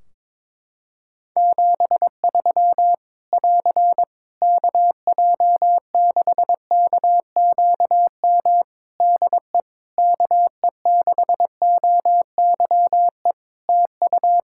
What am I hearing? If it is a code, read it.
Morse code, heard as a keyed tone, about 22 wpm, 700 Hz. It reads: 73 <AR> KJ6KQM DE KE6OYE TU
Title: Practice QSO - Exchange 4 - Ending - 22wpm